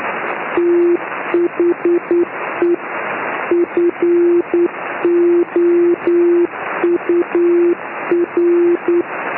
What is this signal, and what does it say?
Morse code, heard as a keyed tone, about 9 wpm, 350 Hz. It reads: THE FOUR
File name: ei4dq / s55zmb beacon